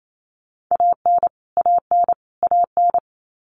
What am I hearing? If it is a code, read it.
Morse code, heard as a keyed tone, about 28 wpm, 700 Hz. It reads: AN AN AN
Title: Top 300 Words Repeated 28wpm